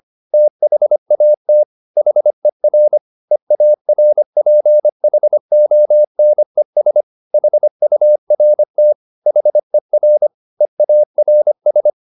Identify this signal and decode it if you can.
Morse code, heard as a keyed tone, about 25 wpm, 600 Hz. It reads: THAT HER EARPHONES HURT HER EARS